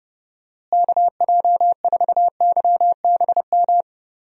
Morse code, heard as a keyed tone, about 30 wpm, 700 Hz. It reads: KJ4YBM
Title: USA Call Signs - 6 Characters - Repeated 2x - 30wpm